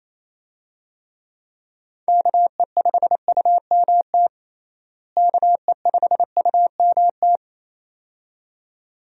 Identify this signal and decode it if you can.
Morse code, heard as a keyed tone, about 28 wpm, 700 Hz. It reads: KE5UMT KE5UMT
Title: USA Call Signs - 6 Characters - Repeated 2x - 28wpm